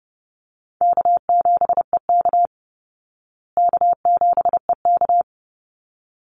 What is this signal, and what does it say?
Morse code, heard as a keyed tone, about 30 wpm, 700 Hz. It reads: K7EK K7EK
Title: Easy Call Signs Repeated 2x - 30wpm